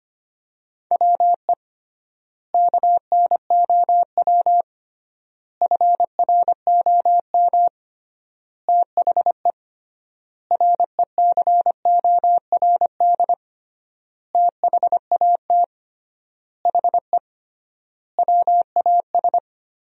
Morse code, heard as a keyed tone, about 25 wpm, 700 Hz. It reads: WE KNOW FROM THE RECORD THAT HE WAS
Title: 3x Word-Spacing - Sentences from Top 500 Words - 25wpm